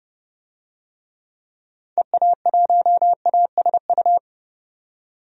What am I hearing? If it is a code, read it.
Morse code, heard as a keyed tone, about 30 wpm, 700 Hz. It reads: EA1ASU
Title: DXCC-100 Call Signs - 6 Characters - 30wpm